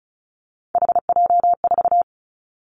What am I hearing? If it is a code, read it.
Morse code, heard as a keyed tone, about 36 wpm, 700 Hz. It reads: HJ4
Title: Callsign Prefixes for Top 100 DXCC - 35wpm